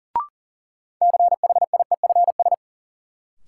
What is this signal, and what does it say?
Morse code, heard as a keyed tone, about 40 wpm, 700 Hz. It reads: CHIEFS
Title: Sets of 5 Words Encore - New Word Drill - 40wpm